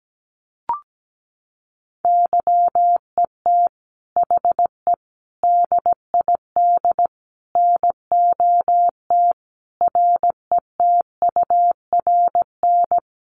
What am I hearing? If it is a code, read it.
Morse code, heard as a keyed tone, about 17 wpm, 700 Hz. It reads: YET HE DID NOT RETURN
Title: Sentences from Top 800 Words 17wpm